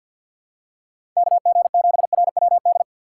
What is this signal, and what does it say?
Morse code, heard as a keyed tone, about 53 wpm, 700 Hz. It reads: KG7RWD
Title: USA Call Signs - 6 Characters - 4x Character Spacing - 50wpm